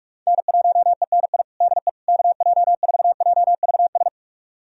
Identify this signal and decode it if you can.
Morse code, heard as a keyed tone, about 44 wpm, 700 Hz. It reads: N1ENI DE KJ4JVS
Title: USA Callsign DE Callsign 45wpm